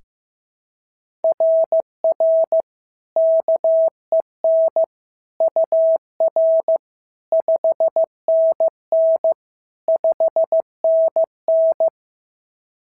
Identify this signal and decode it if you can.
Morse code, heard as a keyed tone, about 14 wpm, 650 Hz. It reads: RR KEN UR 5NN 5NN
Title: SOTA QSO Practice 15wpm